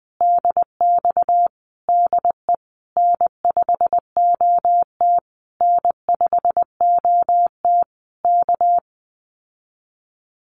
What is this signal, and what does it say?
Morse code, heard as a keyed tone, about 20 wpm, 700 Hz. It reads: DX DE N5OT N5OT K
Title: Common QSO Phrases Encore 20wpm